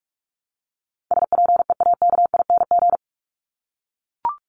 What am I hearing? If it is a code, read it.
Morse code, heard as a keyed tone, about 44 wpm, 700 Hz. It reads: SPEAKING
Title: ING Suffix - Top 100 Words with Base Words 45wpm